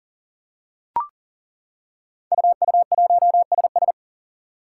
Morse code, heard as a keyed tone, about 40 wpm, 690 Hz. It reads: UU1SS
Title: Five Challenging Letter-Number Combinations 40wpm